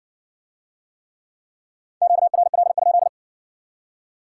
Morse code, heard as a keyed tone, about 54 wpm, 700 Hz. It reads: QRL?